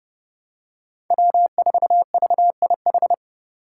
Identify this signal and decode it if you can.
Morse code, heard as a keyed tone, about 30 wpm, 700 Hz. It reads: W4VIH
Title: Code Groups - 5 Letter:Numbers - 30wpm